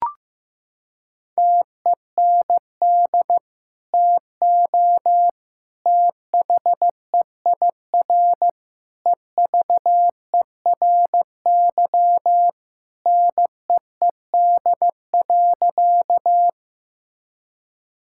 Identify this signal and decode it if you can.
Morse code, heard as a keyed tone, about 15 wpm, 700 Hz. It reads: TEND TO THEIR EVERY NEED.
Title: Sentences from Top 1000 Words - Review Part 5 - 15wpm